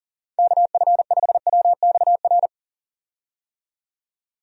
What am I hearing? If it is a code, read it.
Morse code, heard as a keyed tone, about 40 wpm, 700 Hz. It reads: KF5WXR
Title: USA Call Signs - 6 Characters - 40wpm